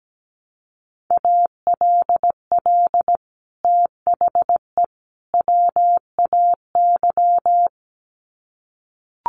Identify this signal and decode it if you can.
Morse code, heard as a keyed tone, about 17 wpm, 700 Hz. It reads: ALL THE WAY
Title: Sets of 3 Words 17wpm